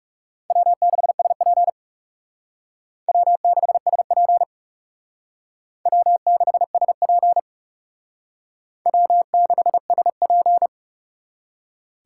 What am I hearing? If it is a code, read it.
Morse code, heard as a keyed tone, about 37 wpm, 700 Hz. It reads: W6SP W6SP W6SP W6SP
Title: Speed Racing - USA 1x2 Call Signs 30wpm